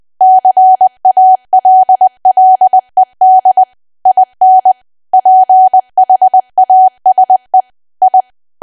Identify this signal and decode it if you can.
Morse code, heard as a keyed tone, about 20 wpm, 750 Hz. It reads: CALLED IN PHASE I
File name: cwa / W1AW-Level3 / 20WPM